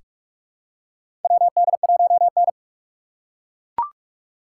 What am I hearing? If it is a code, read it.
Morse code, heard as a keyed tone, about 43 wpm, 690 Hz. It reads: WD1N
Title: USA Call Signs - 4 Characters - 3x Character Spacing - 45wpm